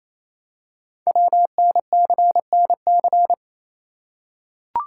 Morse code, heard as a keyed tone, about 29 wpm, 700 Hz. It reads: WNCNC
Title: Code Groups - 5 Letters - 28wpm